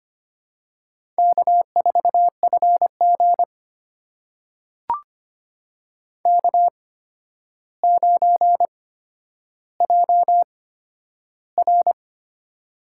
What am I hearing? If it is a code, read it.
Morse code, heard as a keyed tone, about 25 wpm, spaced as for 11 wpm, 700 Hz. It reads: K4FG K9JR
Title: USA Call Signs - 4 Characters - 8x Character Spacing - 25wpm